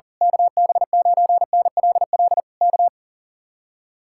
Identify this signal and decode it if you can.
Morse code, heard as a keyed tone, about 38 wpm, 700 Hz. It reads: KB9NLL K
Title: Speed-Racing Common QSO Phrases - 40wpm